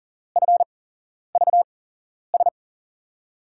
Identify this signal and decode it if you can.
Morse code, heard as a keyed tone, about 39 wpm, spaced as for 11 wpm, 700 Hz. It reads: FVS
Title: USA Call Signs - 6 Characters - 8x Character Spacing - 40wpm